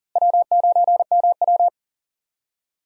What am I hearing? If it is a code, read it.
Morse code, heard as a keyed tone, about 39 wpm, 700 Hz. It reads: W9MW
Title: USA Call Signs - 4 Characters - Repeated 2x - 40wpm